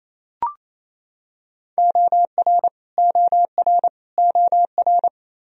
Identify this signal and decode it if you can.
Morse code, heard as a keyed tone, about 28 wpm, 700 Hz. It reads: OR OR OR
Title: Top 100 Words Repeated 28wpm